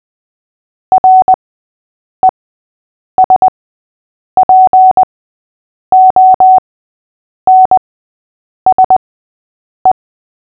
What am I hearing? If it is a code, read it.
Morse code, heard as a keyed tone, about 20 wpm, spaced as for 8 wpm, 750 Hz. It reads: RESPONSE
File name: cw / fists / Short-Story-8-8wpm